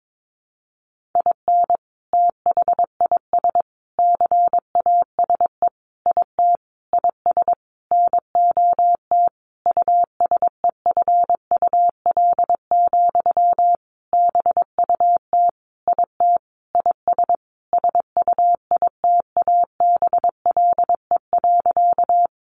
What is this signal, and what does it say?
Morse code, heard as a keyed tone, about 22 wpm, 700 Hz. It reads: IN THIS CASE IT IS NOT USEFUL, BUT IT IS SUITABLE.
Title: Sentences from Top 1400 Words 22wpm